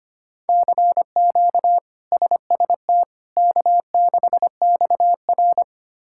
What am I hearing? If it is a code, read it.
Morse code, heard as a keyed tone, about 25 wpm, 700 Hz. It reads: CQ SST K6XR
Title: K1USN SST QSO Practice - 25wpm